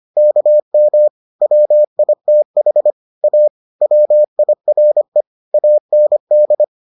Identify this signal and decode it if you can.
Morse code, heard as a keyed tone, about 25 wpm, 600 Hz. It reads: KM WITH A WIRE AND